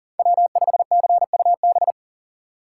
Morse code, heard as a keyed tone, about 40 wpm, 700 Hz. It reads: W5CUB
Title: USA Call Signs - 4x Character Spacing - 40wpm